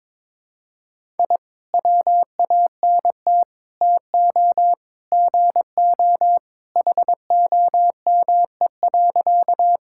Morse code, heard as a keyed tone, about 22 wpm, 700 Hz. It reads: I WANT TO GO HOME.
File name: practice / 22wpm / Sentences with 3-Word Phrases - 22wpm